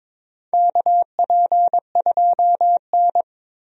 Morse code, heard as a keyed tone, about 23 wpm, 700 Hz. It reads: KP2N
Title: DXCC-100 Call Signs - 4 Characters - 22wpm